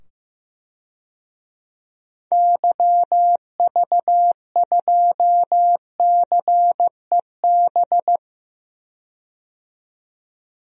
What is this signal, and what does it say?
Morse code, heard as a keyed tone, about 15 wpm, 700 Hz. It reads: YV2CEB
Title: DXCC-100 Call Signs - 6 Characters - 15wpm